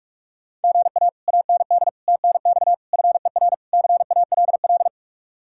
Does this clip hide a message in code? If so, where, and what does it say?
Yes: Morse code from 0.6 to 4.9 s, heard as a keyed tone, about 49 wpm, 700 Hz. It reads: GA AND TNX FER CALL